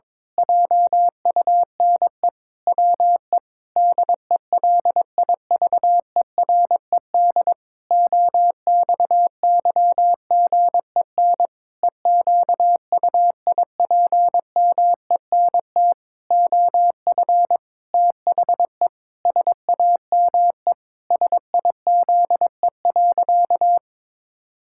Morse code, heard as a keyed tone, about 22 wpm, 700 Hz. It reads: JUNE WE DELIVERED OXYGEN EQUIPMENT OF THE SAME SIZE.
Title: Pangram Challenge v2 - 22wpm